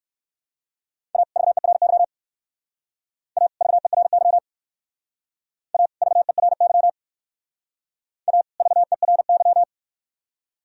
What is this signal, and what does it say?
Morse code, heard as a keyed tone, about 58 wpm, 700 Hz. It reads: A VERY A VERY A VERY A VERY